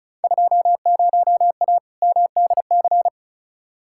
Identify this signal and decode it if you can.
Morse code, heard as a keyed tone, about 36 wpm, 690 Hz. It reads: 20A MDC
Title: Field Day Exchanges Encore - 35wpm